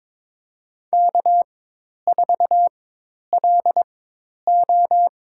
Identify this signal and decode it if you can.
Morse code, heard as a keyed tone, about 22 wpm, spaced as for 10 wpm, 700 Hz. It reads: K4LO
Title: USA Call Signs - 4 Characters - 4x Character Spacing - 22wpm